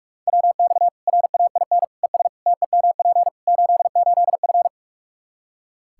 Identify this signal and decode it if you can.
Morse code, heard as a keyed tone, about 46 wpm, 700 Hz. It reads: WX RAIN ES TEMP 88F
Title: Practice QSO - Exchange 2 - Info - 45wpm